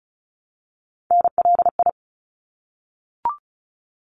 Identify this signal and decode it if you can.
Morse code, heard as a keyed tone, about 34 wpm, 700 Hz. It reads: NLI
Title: ARRL Sections 35wpm